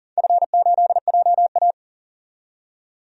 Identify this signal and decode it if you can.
Morse code, heard as a keyed tone, about 42 wpm, 700 Hz. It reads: F8JA
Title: DXCC-100 Call Signs - 4 Characters - 40wpm